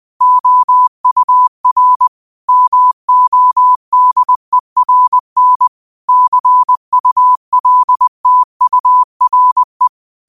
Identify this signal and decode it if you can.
Morse code, heard as a keyed tone, about 20 wpm, 1000 Hz. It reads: OUR MODERN CULTURE
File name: qotdfiles / 20wpm